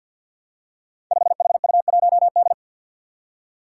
Morse code, heard as a keyed tone, about 47 wpm, 700 Hz. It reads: 5HU1D
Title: Five Challenging Letter-Number Combinations 50wpm